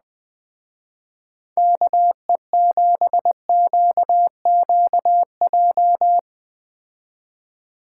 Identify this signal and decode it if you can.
Morse code, heard as a keyed tone, about 20 wpm, 700 Hz. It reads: KE7QQJ